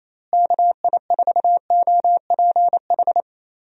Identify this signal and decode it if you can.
Morse code, heard as a keyed tone, about 28 wpm, 700 Hz. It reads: KI4OPH
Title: USA Call Signs - 6 Characters - Repeated 2x - 28wpm